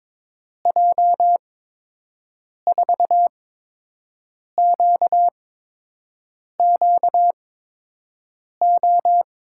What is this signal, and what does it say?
Morse code, heard as a keyed tone, about 22 wpm, spaced as for 6 wpm, 700 Hz. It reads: J4QQO
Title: USA Call Signs - 6 Characters - 8x Character Spacing - 22wpm